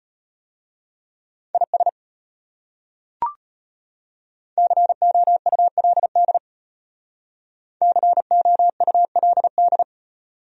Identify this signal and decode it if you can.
Morse code, heard as a keyed tone, about 31 wpm, 700 Hz. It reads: IS COULD COULD